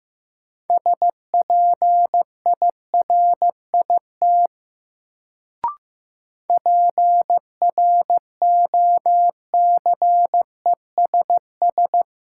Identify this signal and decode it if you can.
Morse code, heard as a keyed tone, about 15 wpm, 700 Hz. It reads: SPIRIT PROCESS
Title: Sentences from Top 1400 Words - New Word Drill Alt - 15wpm